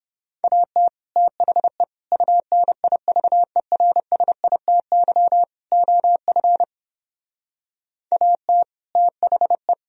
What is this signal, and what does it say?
Morse code, heard as a keyed tone, about 29 wpm, 700 Hz. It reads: AT THE UNIVERSITY OF AT THE